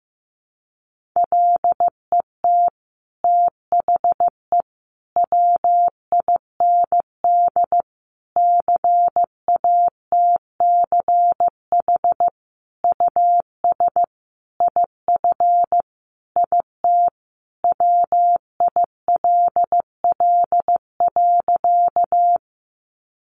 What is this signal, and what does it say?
Morse code, heard as a keyed tone, about 15 wpm, 700 Hz. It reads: LET THE WIND CATCH US IF IT WILL.
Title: Sentences from Top 1500 Words 15wpm